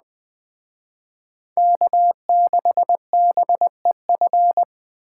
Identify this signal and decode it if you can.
Morse code, heard as a keyed tone, about 20 wpm, 700 Hz. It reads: K6BEF
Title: Speed Racing - USA 1x3 Call Signs 17wpm